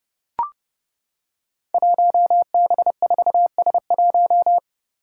Morse code, heard as a keyed tone, about 30 wpm, 700 Hz. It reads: 1B4S1
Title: Five Challenging Letter-Number Combinations 30wpm